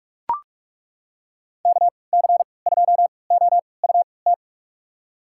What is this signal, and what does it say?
Morse code, heard as a keyed tone, about 42 wpm, spaced as for 25 wpm, 690 Hz. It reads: KC2OUT
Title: USA Call Signs - 6 Characters - 3x Character Spacing - 45wpm